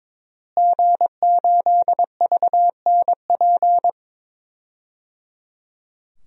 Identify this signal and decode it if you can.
Morse code, heard as a keyed tone, about 22 wpm, 700 Hz. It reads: G8VNP
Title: DXCC-100 Call Signs - 5 Characters - Repeated 3x - 22wpm